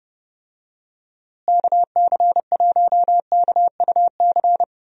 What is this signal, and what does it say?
Morse code, heard as a keyed tone, about 30 wpm, 700 Hz. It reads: KC1KUC
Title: USA Call Signs - 6 Characters - Repeated 3x - 30wpm